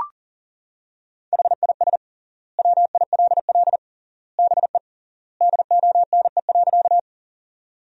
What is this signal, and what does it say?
Morse code, heard as a keyed tone, about 40 wpm, 700 Hz. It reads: HIS WILL BE DONE.
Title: 3x Word-Spacing - Sentences from Top 500 Words - 40wpm